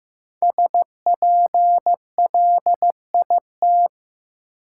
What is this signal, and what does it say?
Morse code, heard as a keyed tone, about 15 wpm, 700 Hz. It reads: SPLIT
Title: Five Letter Words - 15wpm